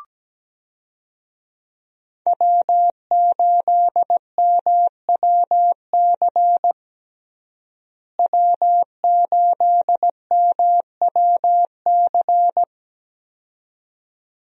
Morse code, heard as a keyed tone, about 17 wpm, 700 Hz. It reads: W8MWC W8MWC
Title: USA Call Signs - 5 Characters - Repeated 2x - 17wpm